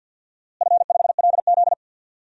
Morse code, heard as a keyed tone, about 47 wpm, 700 Hz. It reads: F5LZ
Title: DXCC-100 Call Signs - 4 Characters - Repeated 2x - 50wpm